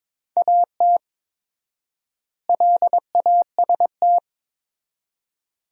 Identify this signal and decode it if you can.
Morse code, heard as a keyed tone, about 22 wpm, 700 Hz. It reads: AT LAST
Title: 4x Word-Spacing - Sentences from Top 300 Words - 22wpm